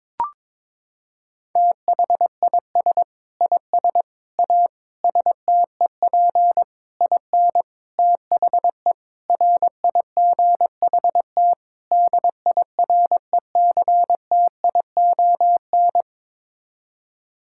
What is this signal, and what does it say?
Morse code, heard as a keyed tone, about 22 wpm, 700 Hz. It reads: THIS IS A STEP IN THE RIGHT DIRECTION
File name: practice / 22wpm / Sets of 8 Words - 22wpm